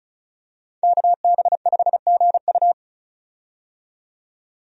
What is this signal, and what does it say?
Morse code, heard as a keyed tone, about 34 wpm, 700 Hz. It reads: KB5GU
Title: USA Call Signs - 5 Characters - 35wpm